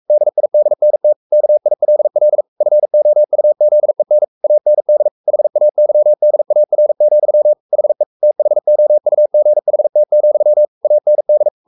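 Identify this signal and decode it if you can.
Morse code, heard as a keyed tone, about 44 wpm, 600 Hz. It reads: DIDNT KILL FOUZEN AND HAYDAR, HE THOUGHT, AND
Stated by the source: Morse code at 43 wpm sending Ken Follett's novel Never in English, from CW med Gnister.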